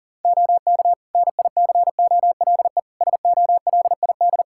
Morse code, heard as a keyed tone, about 40 wpm, 700 Hz. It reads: OK NICOLE SOLID
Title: Practice QSO - Exchange 3 - More Info - 40wpm